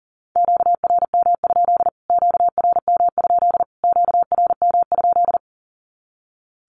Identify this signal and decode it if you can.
Morse code, heard as a keyed tone, about 37 wpm, 700 Hz. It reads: QRM? QRM? QRM?
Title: Q-Codes Repeated 3x 40wpm